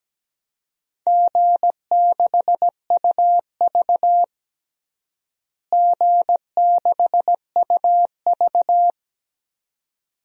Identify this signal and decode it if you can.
Morse code, heard as a keyed tone, about 17 wpm, 700 Hz. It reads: G6UV G6UV
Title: DXCC-100 Call Signs - 4 Characters - Repeated 2x - 17wpm